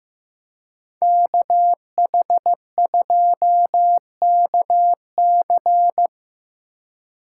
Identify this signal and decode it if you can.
Morse code, heard as a keyed tone, about 15 wpm, 700 Hz. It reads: KH2KC
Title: USA Call Signs - 2x Character Spacing - 15wpm